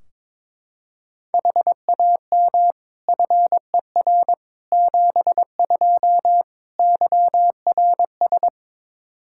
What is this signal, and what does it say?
Morse code, heard as a keyed tone, about 22 wpm, 700 Hz. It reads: HAM FER 72 YRS
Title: K1USN SST QSO Practice 22wpm